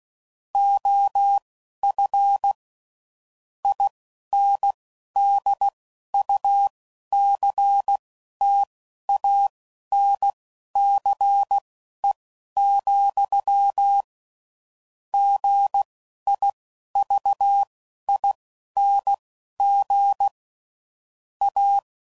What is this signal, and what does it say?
Morse code, heard as a keyed tone, about 16 wpm, spaced as for 11 wpm, 800 Hz. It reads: OF INDUCTANCE, GIVING A
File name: CW Code Groups @10WPM / Technical Subjects / XL